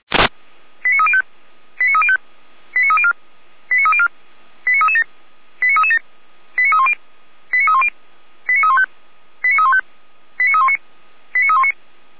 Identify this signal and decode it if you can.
Five-tone selective call (ZVEI), heard as quick runs of five tones, tones of about 70 ms. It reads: 89305 89305 89304 89304 89337 89337 89310 89310 89315 89315 89319 89319